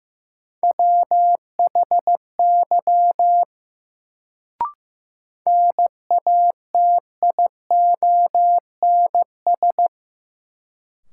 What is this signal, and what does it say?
Morse code, heard as a keyed tone, about 15 wpm, 700 Hz. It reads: WHY NATIONS
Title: Sets of 3 Words - New Word Drill - 15wpm